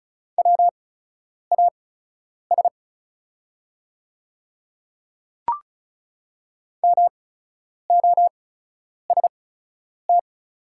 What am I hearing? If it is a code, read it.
Morse code, heard as a keyed tone, about 36 wpm, spaced as for 10 wpm, 700 Hz. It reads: WAS MOST
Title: 8x Character Spacing - Top 100 Words - 35wpm